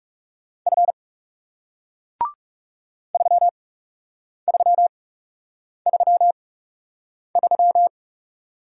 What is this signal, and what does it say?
Morse code, heard as a keyed tone, about 40 wpm, spaced as for 8 wpm, 700 Hz. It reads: F 3333